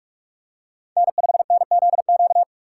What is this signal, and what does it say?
Morse code, heard as a keyed tone, about 44 wpm, 700 Hz. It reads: N5NZQ
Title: USA Call Signs - 5 Characters - 3x Character Spacing - 45wpm